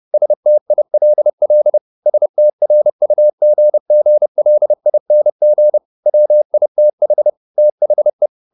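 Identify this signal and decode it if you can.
Morse code, heard as a keyed tone, about 30 wpm, 600 Hz. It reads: STILL STRUGGLING WITH THE
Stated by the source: Morse code at 30 wpm, the novel Never by Ken Follett in English, from CW med Gnister.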